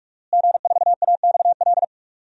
Morse code, heard as a keyed tone, about 46 wpm, 700 Hz. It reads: G4AXL
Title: DXCC-100 Call Signs - 5 Characters - Repeated 3x - 45wpm